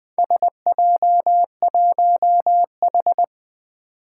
Morse code, heard as a keyed tone, about 20 wpm, 700 Hz. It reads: SJ1H
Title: Four Challenging Letter-Number Combinations 20wpm